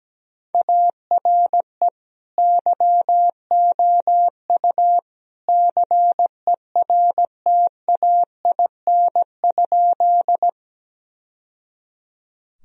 Morse code, heard as a keyed tone, about 17 wpm, 700 Hz. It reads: ARE YOU CERTAIN?
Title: Sentences from Top 1000 Words - Review Part 2 - 17wpm